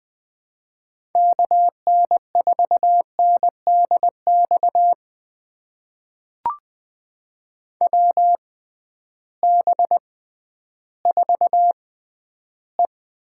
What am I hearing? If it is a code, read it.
Morse code, heard as a keyed tone, about 20 wpm, spaced as for 12 wpm, 700 Hz. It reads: KN4NDX WB4E
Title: USA Call Signs - 6 Characters - 6x Character Spacing - 20wpm